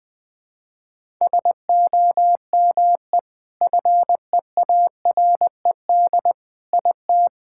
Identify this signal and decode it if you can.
Morse code, heard as a keyed tone, about 20 wpm, 700 Hz. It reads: SOME FEARED IT
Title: Sentences with 5-Word Phrases - 20wpm